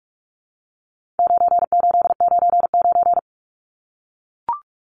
Morse code, heard as a keyed tone, about 44 wpm, 700 Hz. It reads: 9899
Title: Numbers - 4-Digit 45wpm